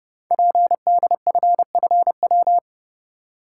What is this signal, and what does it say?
Morse code, heard as a keyed tone, about 30 wpm, 700 Hz. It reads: PDFFW
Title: Code Groups - 5 Letter:Numbers - 30wpm